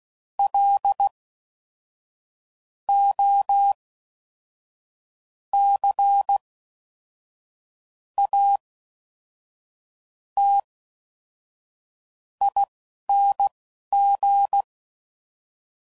Morse code, heard as a keyed tone, about 16 wpm, spaced as for 5 wpm, 800 Hz. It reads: LOCATING